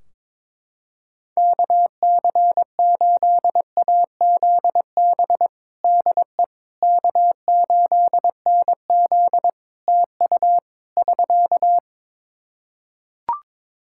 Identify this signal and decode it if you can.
Morse code, heard as a keyed tone, about 21 wpm, 700 Hz. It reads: KC8AZB DE K8NZ TU <SK>
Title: Practice QSO - Exchange 4 - Ending - 22wpm